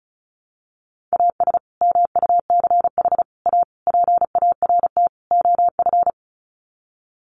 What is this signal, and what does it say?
Morse code, heard as a keyed tone, about 34 wpm, 700 Hz. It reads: AS MUCH A PART OF
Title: Sets of 5 Words Encore2 35wpm